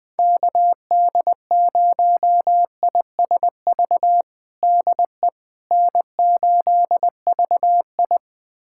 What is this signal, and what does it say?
Morse code, heard as a keyed tone, about 20 wpm, 700 Hz. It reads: KD0ISV DE N8VI
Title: USA Callsign DE Callsign 20wpm